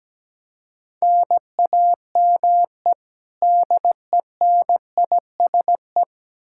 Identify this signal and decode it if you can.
Morse code, heard as a keyed tone, about 17 wpm, 700 Hz. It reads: NAME DENISE
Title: Practice QSO - Exchange 1 - Beginning - 17wpm